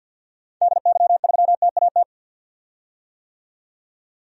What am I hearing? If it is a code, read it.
Morse code, heard as a keyed tone, about 52 wpm, 700 Hz. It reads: DY3TAT
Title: DXCC-100 Call Signs - 6 Characters - 50wpm